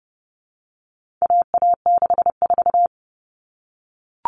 Morse code, heard as a keyed tone, about 30 wpm, 700 Hz. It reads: AA64
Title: All Cut Numbers - 4-Digit 30wpm